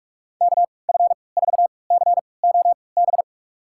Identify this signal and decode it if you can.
Morse code, heard as a keyed tone, about 46 wpm, spaced as for 25 wpm, 700 Hz. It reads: KF4COB
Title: USA Call Signs - 6 Characters - 3x Character Spacing - 45wpm